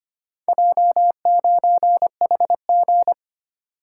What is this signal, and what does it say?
Morse code, heard as a keyed tone, about 25 wpm, 690 Hz. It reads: J9HG
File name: practice / 25wpm / Four Letter-Number Combinations - 25wpm